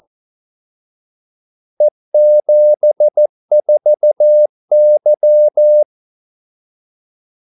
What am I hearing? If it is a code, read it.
Morse code, heard as a keyed tone, about 14 wpm, 600 Hz. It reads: E74Y
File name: Callsigns-14wpm